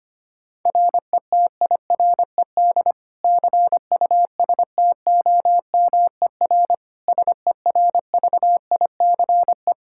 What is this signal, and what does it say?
Morse code, heard as a keyed tone, about 25 wpm, 700 Hz. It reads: RETIRED CUSTOMER SERVICE
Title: Practice QSO - Exchange 3 - More Info - 25wpm